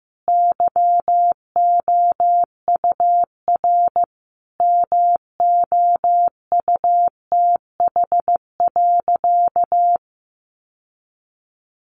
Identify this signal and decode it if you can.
Morse code, heard as a keyed tone, about 15 wpm, 700 Hz. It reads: YOUR MOUTH.
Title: Sentences from Top 1500 Words 15wpm